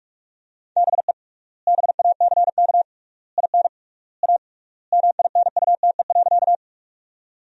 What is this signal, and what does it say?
Morse code, heard as a keyed tone, about 47 wpm, 700 Hz. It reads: BE BACK IN A MINUTE.
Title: 3x Word-Spacing - Sentences from Top 500 Words - 45wpm